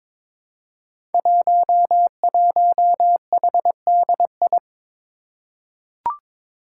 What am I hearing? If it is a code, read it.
Morse code, heard as a keyed tone, about 23 wpm, 700 Hz. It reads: 11HDI
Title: Five Challenging Letter-Number Combinations 22wpm